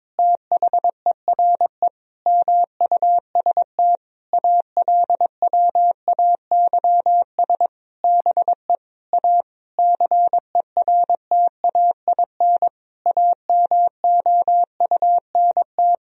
Morse code, heard as a keyed tone, about 22 wpm, 700 Hz. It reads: THERE MUST ALWAYS BE A CERTAIN AMOUNT
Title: Sentences from Top 1000 Words - Review Part 4 - 22wpm